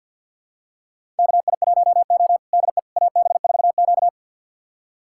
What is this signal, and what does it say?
Morse code, heard as a keyed tone, about 48 wpm, 700 Hz. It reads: KI1O DE AB4Q
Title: USA Callsign DE Callsign - 4 Characters - 50wpm